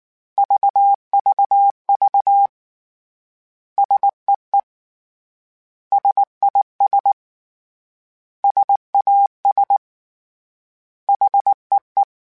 Morse code, heard as a keyed tone, about 19 wpm, 800 Hz. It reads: VVV SEE SIS SAS HEE
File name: S vs H / S vs H mp3s / S vs H @18